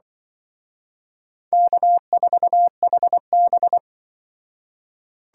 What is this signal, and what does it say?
Morse code, heard as a keyed tone, about 24 wpm, 700 Hz. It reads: K4HB
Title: Speed Racing - USA Call Signs 20wpm